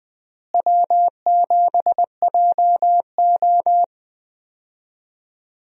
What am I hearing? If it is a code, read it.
Morse code, heard as a keyed tone, about 20 wpm, 700 Hz. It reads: W7JO
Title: USA Call Signs - 4 Characters - Repeated 2x - 20wpm